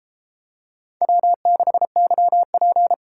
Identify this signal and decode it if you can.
Morse code, heard as a keyed tone, about 31 wpm, 700 Hz. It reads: W6YP